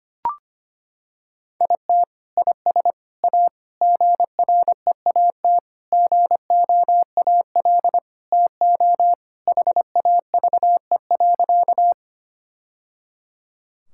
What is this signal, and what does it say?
Morse code, heard as a keyed tone, about 25 wpm, 700 Hz. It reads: IT IS A GREAT GOAL TO HAVE.
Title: Sentences from Top 1000 Words - Review Part 5 - 25wpm